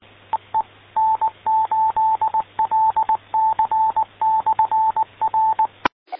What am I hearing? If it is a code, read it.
Morse code, heard as a keyed tone, about 20 wpm, 880 Hz. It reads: EE N8LC/R